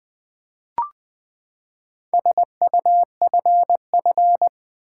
Morse code, heard as a keyed tone, about 20 wpm, 700 Hz. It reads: SUFF
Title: Build A Word - 4 Letters At A Time - 12-Letter Words - 20wpm